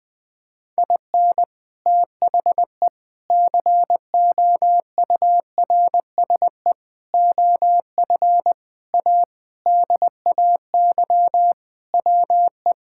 Morse code, heard as a keyed tone, about 21 wpm, 700 Hz. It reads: IN THE COURSE OF A DAY WE
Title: Sentences with 3-Word Phrases - Spoken Beginning - 20wpm